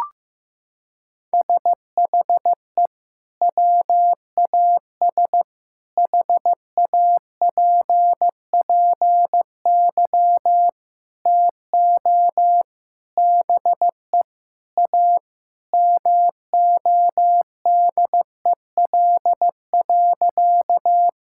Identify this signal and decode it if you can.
Morse code, heard as a keyed tone, about 15 wpm, 700 Hz. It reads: SHE WAS HAPPY TO BE A MODEL.
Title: Sentences from Top 1000 Words - Review Part 5 - 15wpm